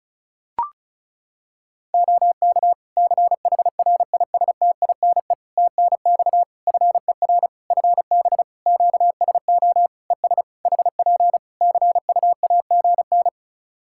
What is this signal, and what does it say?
Morse code, heard as a keyed tone, about 36 wpm, 700 Hz. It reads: OK CHRISTINE TNX FER FB QSO ES HP CUAGN
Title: Practice QSO - Exchange 4 - Ending - 35wpm